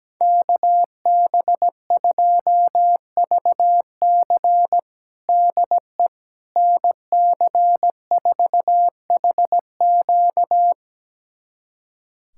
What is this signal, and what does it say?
Morse code, heard as a keyed tone, about 17 wpm, 700 Hz. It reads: KB2VC DE NC4HQ
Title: USA Callsign DE Callsign - 5 Characters - 17wpm